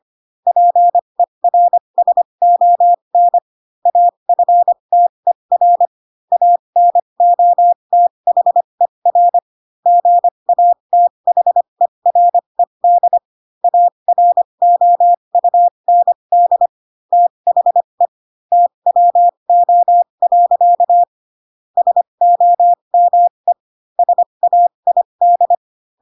Morse code, heard as a keyed tone, about 25 wpm, spaced as for 22 wpm, 700 Hz. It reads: PERSON AFTER ANOTHER GATHERED AROUND THE TWO. SOME SAID
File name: Pinocchio / Pinocchio ch3 18 25 ik1hge